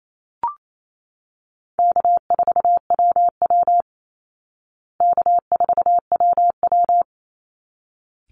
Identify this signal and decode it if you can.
Morse code, heard as a keyed tone, about 28 wpm, 700 Hz. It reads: K4WW K4WW
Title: Easy Call Signs Repeated 2x - 28wpm